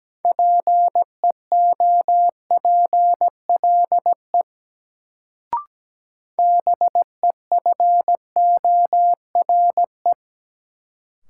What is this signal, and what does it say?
Morse code, heard as a keyed tone, about 17 wpm, 700 Hz. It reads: PEOPLE BEFORE
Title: Sentences from Top 200 Words - New Word Drill - 17wpm